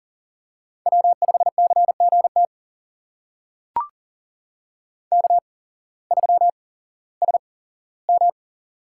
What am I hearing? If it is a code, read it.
Morse code, heard as a keyed tone, about 38 wpm, spaced as for 18 wpm, 700 Hz. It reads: W5CGT K3SM
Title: USA Call Signs - 8x Character Spacing - 40wpm